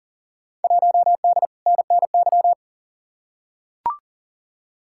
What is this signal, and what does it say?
Morse code, heard as a keyed tone, about 40 wpm, 700 Hz. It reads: 1D NNY
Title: Field Day Exchanges - 40wpm